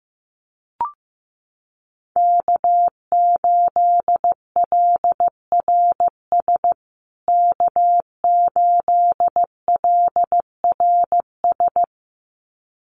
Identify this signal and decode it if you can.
Morse code, heard as a keyed tone, about 15 wpm, 700 Hz. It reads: K8LRS K8LRS
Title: USA Call Signs Repeated 2x 15wpm